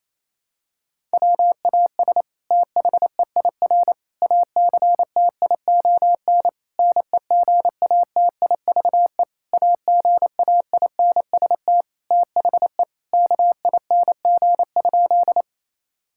Morse code, heard as a keyed tone, about 27 wpm, 700 Hz. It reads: WAS THEIR ACTION NEGATIVE AGAINST THE KING?
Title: Sentences from Top 1000 Words - Review Part 6 - 28wpm